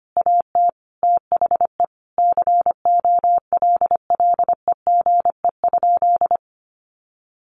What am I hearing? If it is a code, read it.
Morse code, heard as a keyed tone, about 25 wpm, 700 Hz. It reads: AT THE COLLEGE?
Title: Sentences from Top 1300 Words 25wpm